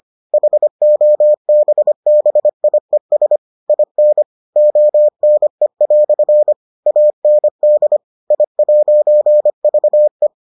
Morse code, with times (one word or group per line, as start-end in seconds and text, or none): 0.3-3.4 HOBBIES
3.7-4.2 IN
4.6-6.5 ONE"
6.9-8.0 AND
8.3-10.3 I'VE